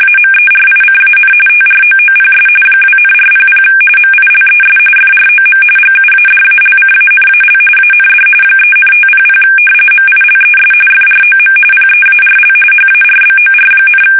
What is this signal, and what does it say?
RTTY, about 45.45 baud. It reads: AMERICAN TROOPS 2,400 MILES FROM GUADALCANAL AND 3,500 MILES FROM PEARL HARBOR,
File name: fsk850 / battleshipnj